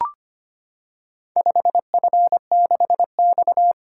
Morse code, heard as a keyed tone, about 25 wpm, 700 Hz. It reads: 5F6X
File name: practice / 25wpm / Four Letter-Number Combinations - 25wpm